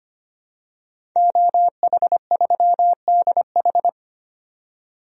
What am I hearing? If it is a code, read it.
Morse code, heard as a keyed tone, about 25 wpm, 700 Hz. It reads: OH3DH
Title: DXCC-100 Call Signs - 5 Characters - 25wpm